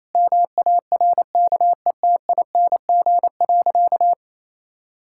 Morse code, heard as a keyed tone, about 28 wpm, 700 Hz. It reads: MARKETING.
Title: Sentences from Top 1000 Words - Review Part 6 - 28wpm